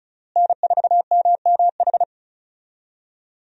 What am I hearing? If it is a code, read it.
Morse code, heard as a keyed tone, about 36 wpm, 700 Hz. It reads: N4MMH
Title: USA Call Signs - 5 Characters - Repeated 3x - 35wpm